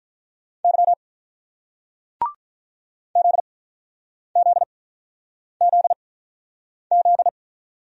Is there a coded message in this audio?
Yes: Morse code from 0.6 to 7.3 s, heard as a keyed tone, about 48 wpm, spaced as for 9 wpm, 690 Hz. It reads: Y ZZZZ